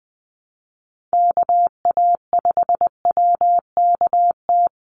Morse code, heard as a keyed tone, about 20 wpm, 700 Hz. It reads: KA5WKT
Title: USA Call Signs 20wpm